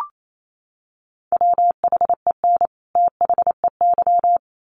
Morse code, heard as a keyed tone, about 28 wpm, 700 Hz. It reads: WHEN THEY
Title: Sets of 2 Words 28wpm